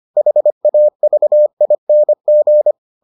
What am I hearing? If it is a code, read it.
Morse code, heard as a keyed tone, about 25 wpm, 600 Hz. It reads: HAVING